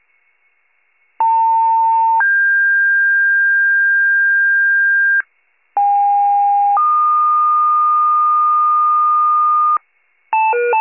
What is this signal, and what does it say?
Two-tone page, heard as a one-second tone, then a three-second tone, about 900 then 1600 Hz, about 810 then 1220 Hz, with noise. It is heard from 1.2 to 9.8 s.
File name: wcalert